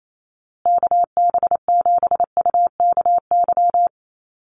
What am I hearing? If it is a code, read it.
Morse code, heard as a keyed tone, about 28 wpm, 700 Hz. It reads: KB7UKY
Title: USA Call Signs Repeated 2x 28wpm